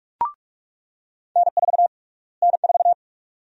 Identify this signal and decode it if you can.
Morse code, heard as a keyed tone, about 46 wpm, 700 Hz. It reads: N4 N4
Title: USA Call Sign Prefixes - Repeated 2x - 45wpm